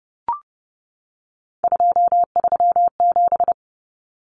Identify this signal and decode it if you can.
Morse code, heard as a keyed tone, about 30 wpm, 700 Hz. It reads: 237
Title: Numbers - 3-Digit 30wpm